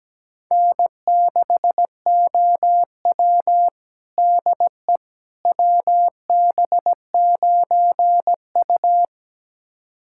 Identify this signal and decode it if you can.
Morse code, heard as a keyed tone, about 17 wpm, 700 Hz. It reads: N6OW DE WB9U
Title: USA Callsign DE Callsign - 4 Characters - 17wpm